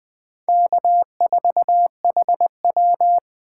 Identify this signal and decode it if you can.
Morse code, heard as a keyed tone, about 24 wpm, 700 Hz. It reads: K4HW
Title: Speed Racing - USA 1x2 Call Signs 20wpm